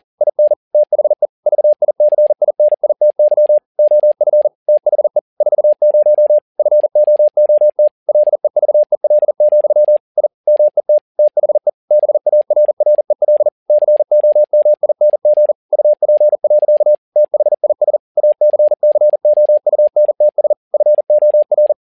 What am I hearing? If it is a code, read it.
Morse code, heard as a keyed tone, about 40 wpm, 600 Hz. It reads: IN THE VICINITY OF THE 40 FOOT LEVEL, I MET THE BARREL COMING UP. THIS ACCOUNTS FOR
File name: learning center / tower trouble / tower touble 40wpm